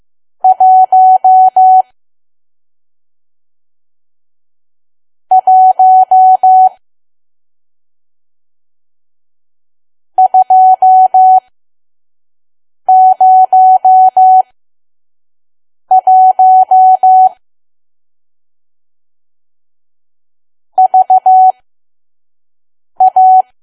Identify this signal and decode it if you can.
Morse code, heard as a keyed tone, about 15 wpm, spaced as for 5 wpm, 750 Hz. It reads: 1 1 201 VA
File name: Archive / 5 WPM files / PM